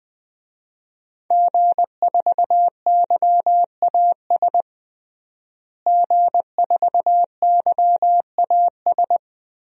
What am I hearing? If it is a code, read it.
Morse code, heard as a keyed tone, about 20 wpm, 700 Hz. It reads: G4YAS G4YAS
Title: DXCC-100 Call Signs - 5 Characters - Repeated 2x - 20wpm